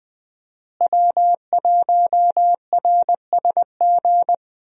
Morse code, heard as a keyed tone, about 20 wpm, 700 Hz. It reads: W1RSG
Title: USA Call Signs - 5 Characters - Repeated 3x - 20wpm